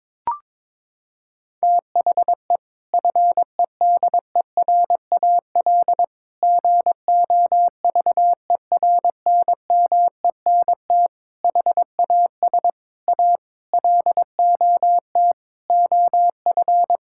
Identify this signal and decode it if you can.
Morse code, heard as a keyed tone, about 22 wpm, 700 Hz. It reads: THE FEDERAL GOVERNMENT HAS A LOT OF
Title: Sentences from Top 1000 Words - Review Part 5 - 22wpm